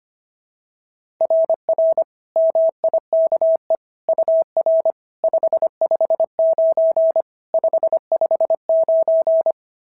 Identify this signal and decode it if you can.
Morse code, heard as a keyed tone, about 22 wpm, 650 Hz. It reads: RR MIKE UR 559 559
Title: SOTA QSO Practice 25wpm